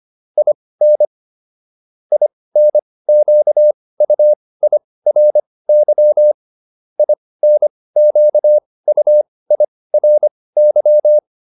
Morse code, heard as a keyed tone, about 25 wpm, spaced as for 18 wpm, 600 Hz. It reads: IN INQUIRY INQUIRY